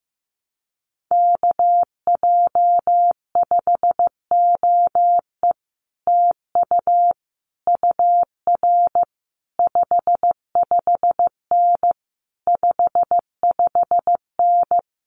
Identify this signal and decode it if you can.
Morse code, heard as a keyed tone, about 15 wpm, 700 Hz. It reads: KJ5OE TU UR 55N 55N
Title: Common QSO Phrases Encore 15wpm